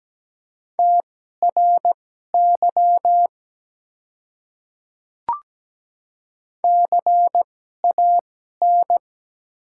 Morse code, heard as a keyed tone, about 17 wpm, spaced as for 12 wpm, 690 Hz. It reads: TRY CAN
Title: 2x Character Spacing - Top 300 Words - 17wpm